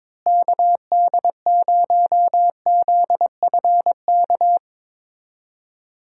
Morse code, heard as a keyed tone, about 22 wpm, 700 Hz. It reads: KD0ZFK
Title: USA Call Signs - 6 Characters - Repeated 3x - 22wpm